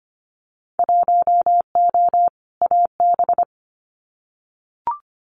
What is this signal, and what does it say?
Morse code, heard as a keyed tone, about 25 wpm, 700 Hz. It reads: 1O AB
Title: Winter Field Day Exchanges Encore 25wpm